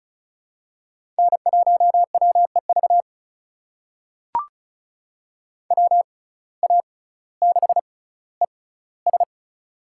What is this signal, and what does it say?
Morse code, heard as a keyed tone, about 36 wpm, spaced as for 18 wpm, 700 Hz. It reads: N1WEV WA6ES
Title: USA Call Signs - 5 Characters - 6x Character Spacing - 35wpm